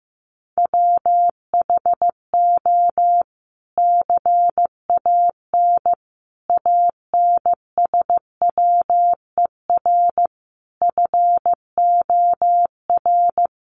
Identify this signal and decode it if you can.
Morse code, heard as a keyed tone, about 15 wpm, 700 Hz. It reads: WHO CAN ANSWER FOR
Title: Sentences from Top 300 Words 15wpm